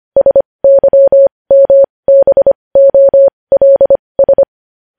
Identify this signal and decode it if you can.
Morse code, heard as a keyed tone, about 25 wpm, spaced as for 20 wpm, 550 Hz. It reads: SYMBOLS
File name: podcast / News headlines-1x-25x20